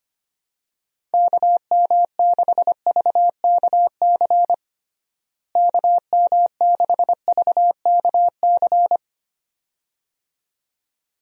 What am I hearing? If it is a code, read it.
Morse code, heard as a keyed tone, about 25 wpm, 700 Hz. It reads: KM6VKC KM6VKC
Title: USA Call Signs - 6 Characters - Repeated 2x - 25wpm